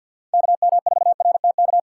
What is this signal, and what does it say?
Morse code, heard as a keyed tone, about 52 wpm, 700 Hz. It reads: KM4RTK
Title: USA Call Signs - 6x Character Spacing - 50wpm